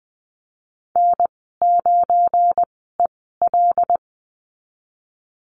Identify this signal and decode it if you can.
Morse code, heard as a keyed tone, about 20 wpm, spaced as for 15 wpm, 700 Hz. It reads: N9EL
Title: 173 Lesson 28g - Number 9 - Call Signs 20wpm